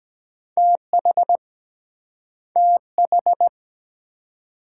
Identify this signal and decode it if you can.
Morse code, heard as a keyed tone, about 23 wpm, 700 Hz. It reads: TH TH